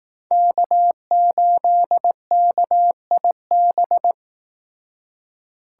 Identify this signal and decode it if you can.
Morse code, heard as a keyed tone, about 18 wpm, 700 Hz. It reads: K8KIB
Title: Speed Racing - USA 1x3 Call Signs 15wpm